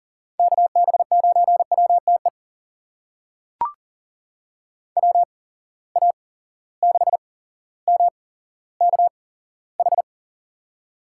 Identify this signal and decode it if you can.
Morse code, heard as a keyed tone, about 40 wpm, spaced as for 17 wpm, 700 Hz. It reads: KB9WTE WA6MKH
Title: USA Call Signs - 6 Characters - 8x Character Spacing - 40wpm